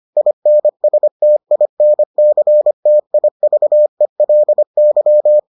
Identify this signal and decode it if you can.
Morse code, heard as a keyed tone, about 25 wpm, 600 Hz. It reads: INSTINCTIVELY